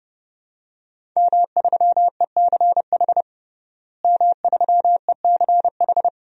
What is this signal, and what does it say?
Morse code, heard as a keyed tone, about 30 wpm, 700 Hz. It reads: M3ECH M3ECH
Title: DXCC-100 Call Signs - 5 Characters - Repeated 2x - 30wpm